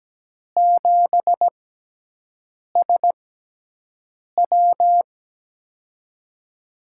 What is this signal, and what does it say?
Morse code, heard as a keyed tone, about 17 wpm, spaced as for 6 wpm, 700 Hz. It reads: 7SW
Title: USA Call Signs - 6x Character Spacing - 17wpm